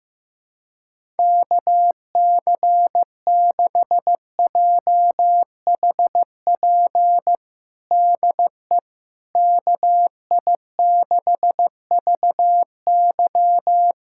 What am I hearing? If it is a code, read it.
Morse code, heard as a keyed tone, about 16 wpm, 700 Hz. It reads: KC6JHP DE KI6VY
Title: USA Callsign DE Callsign 15wpm